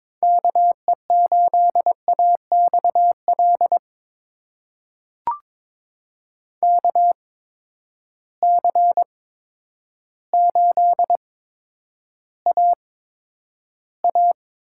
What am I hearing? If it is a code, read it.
Morse code, heard as a keyed tone, about 21 wpm, spaced as for 10 wpm, 700 Hz. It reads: KE8AXL KC8AA
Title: USA Call Signs - 6 Characters - 8x Character Spacing - 22wpm